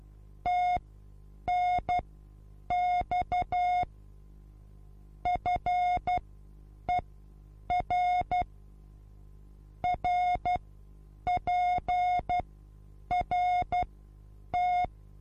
Morse code, heard as a keyed tone, about 12 wpm, spaced as for 8 wpm, 720 Hz. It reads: TNX FER RPRT
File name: Onlinelehrgang / morsen / DJ4UF Morsen23